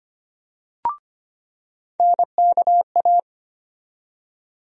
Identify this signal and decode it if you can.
Morse code, heard as a keyed tone, about 25 wpm, 700 Hz. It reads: NKA
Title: USA Call Sign Postfixes - 3 Letters 25wpm